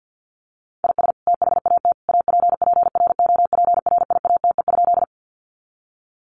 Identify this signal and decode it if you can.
Morse code, heard as a keyed tone, about 49 wpm, 700 Hz. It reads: IS THAT APPROPRIATE?
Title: Sentences from Top 900 Words 50wpm